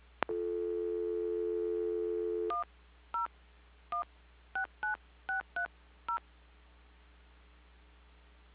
DTMF, heard as telephone touch tones, tones of about 130 ms. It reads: 10169630
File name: recordings / PortlandUnion Millennium